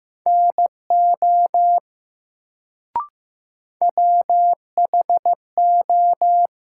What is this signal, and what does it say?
Morse code, heard as a keyed tone, about 15 wpm, 700 Hz. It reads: NO WHO
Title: Sentences from Top 100 Words - New Word Drill - 15wpm